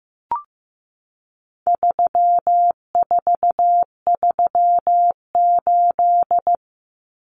Morse code, heard as a keyed tone, about 15 wpm, 700 Hz. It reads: 3438
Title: Numbers - 4-Digit 15wpm